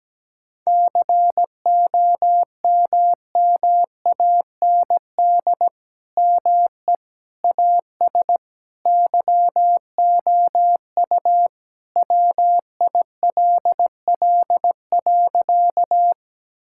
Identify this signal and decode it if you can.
Morse code, heard as a keyed tone, about 17 wpm, 700 Hz. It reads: COMMAND ME AS YOU WILL.
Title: Sentences from Top 1600 Words 17wpm